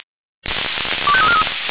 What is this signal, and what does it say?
Five-tone selective call (ZVEI), heard as quick runs of five tones, tones of about 70 ms. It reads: 25343